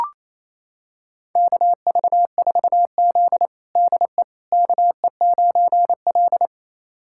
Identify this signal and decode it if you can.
Morse code, heard as a keyed tone, about 28 wpm, 700 Hz. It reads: KV4Z DE KE9L
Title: USA Callsign DE Callsign - 4 Characters - 28wpm